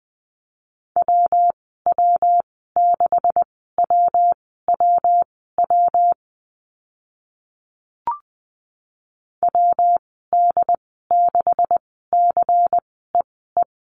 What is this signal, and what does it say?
Morse code, heard as a keyed tone, about 20 wpm, spaced as for 14 wpm, 700 Hz. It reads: WW6WWW WD6CEE
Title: 140 Lesson 23g - Letter f - Call Signs 20wpm